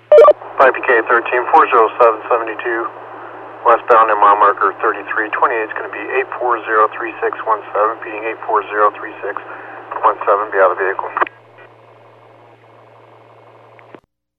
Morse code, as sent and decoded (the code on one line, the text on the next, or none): none